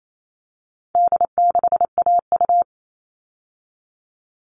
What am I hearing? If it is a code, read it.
Morse code, heard as a keyed tone, about 28 wpm, 700 Hz. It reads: D6AU
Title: All Cut Numbers - 4-Digit 28wpm